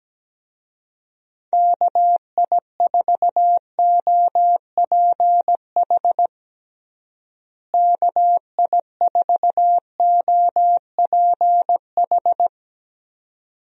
Morse code, heard as a keyed tone, about 17 wpm, 700 Hz. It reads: KI4OPH KI4OPH
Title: USA Call Signs - 6 Characters - Repeated 2x - 17wpm